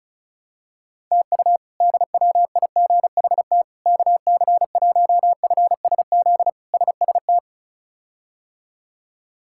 Morse code, heard as a keyed tone, about 34 wpm, 700 Hz. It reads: TU DWIGHT KC1FSZ SST
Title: K1USN SST QSO Practice - 35wpm